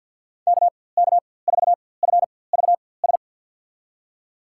Morse code, heard as a keyed tone, about 49 wpm, spaced as for 23 wpm, 700 Hz. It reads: KK4FVS
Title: USA Call Signs - 6 Characters - 4x Character Spacing - 50wpm